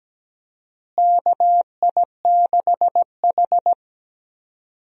Morse code, heard as a keyed tone, about 17 wpm, 700 Hz. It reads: KI6H
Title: USA Call Signs - 4 Characters - Repeated 3x - 17wpm